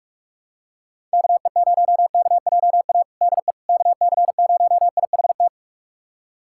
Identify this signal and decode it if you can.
Morse code, heard as a keyed tone, about 46 wpm, 700 Hz. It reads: KE0KJA DE KC0IHT
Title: USA Callsign DE Callsign - 6 Characters - 45wpm